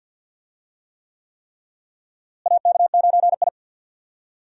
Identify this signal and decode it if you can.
Morse code, heard as a keyed tone, about 46 wpm, 700 Hz. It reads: AK9I
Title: USA Call Signs - 4 Characters - Repeated 3x - 50wpm